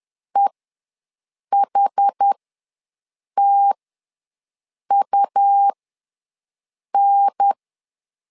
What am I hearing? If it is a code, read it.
Morse code, heard as a keyed tone, about 10 wpm, spaced as for 6 wpm, 800 Hz. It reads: EHTUN
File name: LEZIONI / 4 Q1S ws 10wpm 2208